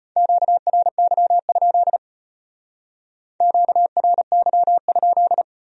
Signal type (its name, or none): Morse code